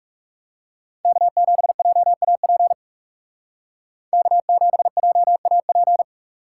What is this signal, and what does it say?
Morse code, heard as a keyed tone, about 35 wpm, 700 Hz. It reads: K7JAP K7JAP
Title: Speed Racing - USA 1x3 Call Signs 30wpm